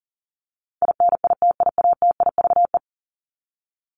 Morse code, heard as a keyed tone, about 38 wpm, 700 Hz. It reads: INITIATIVE
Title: Top 1751-2000 Words 40wpm